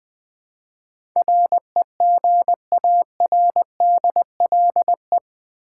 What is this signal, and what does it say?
Morse code, heard as a keyed tone, about 20 wpm, 700 Hz. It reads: REGARDLE
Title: Build A Word - 4 Letters At A Time - 10-Letter Words - 20wpm